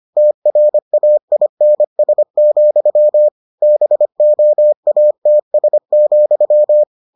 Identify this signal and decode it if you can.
Morse code, heard as a keyed tone, about 25 wpm, 600 Hz. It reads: TRAINS, BOATS,